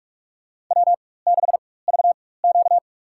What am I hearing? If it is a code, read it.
Morse code, heard as a keyed tone, about 47 wpm, spaced as for 21 wpm, 700 Hz. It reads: W6VQ
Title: USA Call Signs - 4x Character Spacing - 45wpm